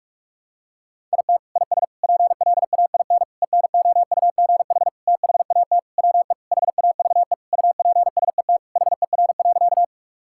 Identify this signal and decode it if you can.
Morse code, heard as a keyed tone, about 41 wpm, 700 Hz. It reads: IT IS PLAIN ENOUGH THAT WE HAVE UPSET HER.
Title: Sentences from Top 1000 Words - Review Part 5 - 45wpm